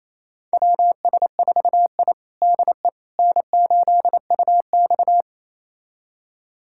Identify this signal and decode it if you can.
Morse code, heard as a keyed tone, about 28 wpm, 700 Hz. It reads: WS4I DE N8UX
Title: USA Callsign DE Callsign - 4 Characters - 28wpm